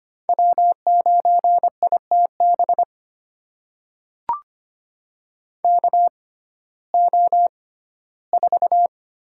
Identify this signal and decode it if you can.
Morse code, heard as a keyed tone, about 25 wpm, spaced as for 15 wpm, 700 Hz. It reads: W9ITB KO4
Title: USA Call Signs - 6x Character Spacing - 25wpm